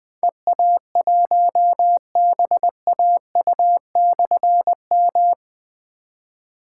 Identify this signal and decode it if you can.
Morse code, heard as a keyed tone, about 20 wpm, 700 Hz. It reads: EA1BAU/M
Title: Difficult Call Signs 20wpm